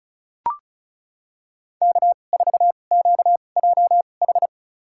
Morse code, heard as a keyed tone, about 36 wpm, spaced as for 25 wpm, 700 Hz. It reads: K4QJH
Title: USA Call Signs - 2x Character Spacing - 35wpm